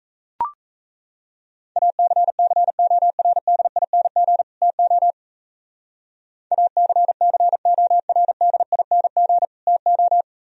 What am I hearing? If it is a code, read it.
Morse code, heard as a keyed tone, about 34 wpm, 700 Hz. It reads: ACCORDING TO ACCORDING TO
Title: Speed-Racing - Sets of 2 Words 28wpm